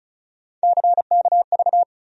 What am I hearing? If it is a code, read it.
Morse code, heard as a keyed tone, about 36 wpm, 700 Hz. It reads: CKV
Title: USA Call Sign Postfixes - 3 Letters 35wpm